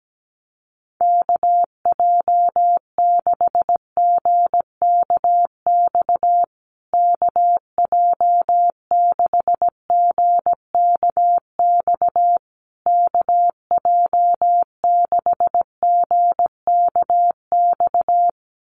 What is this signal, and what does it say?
Morse code, heard as a keyed tone, about 17 wpm, 700 Hz. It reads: KJ6GKX KJ6GKX KJ6GKX
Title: USA Call Signs Repeated 3x 17wpm